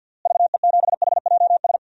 Morse code, heard as a keyed tone, about 51 wpm, 700 Hz. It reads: VE7HJS
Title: DXCC-100 Call Signs - 6 Characters - Repeated 2x - 50wpm